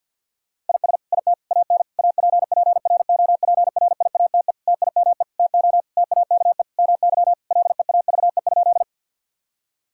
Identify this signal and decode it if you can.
Morse code, heard as a keyed tone, about 50 wpm, 700 Hz. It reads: IS IT AN APPROPRIATE TIME TO TAKE MY LEAVE?
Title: Sentences from Top 1000 Words - Review Part 5 - 50wpm